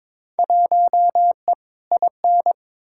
Morse code, heard as a keyed tone, about 21 wpm, 700 Hz. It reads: 1E IN
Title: Field Day Exchanges - 22wpm